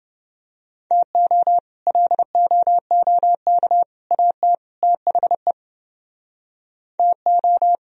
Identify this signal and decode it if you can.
Morse code, heard as a keyed tone, about 30 wpm, 700 Hz. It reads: TO LOOK AT THE TO